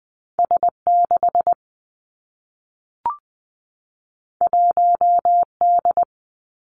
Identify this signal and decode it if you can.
Morse code, heard as a keyed tone, about 19 wpm, 700 Hz. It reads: S 6 1 D
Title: Two Challenging Letter-Number Combinations 20wpm